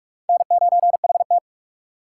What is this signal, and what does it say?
Morse code, heard as a keyed tone, about 60 wpm, spaced as for 52 wpm, 700 Hz. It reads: N9HT
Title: Speed Racing - USA 1x2 Call Signs 45wpm